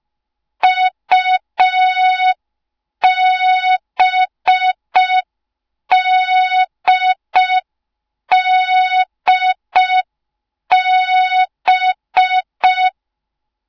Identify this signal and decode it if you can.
Morse code, heard as a keyed tone, about 5 wpm, 3700 Hz. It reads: UBDDB
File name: Morze-HA2MN / hang / 06BDU HA2MN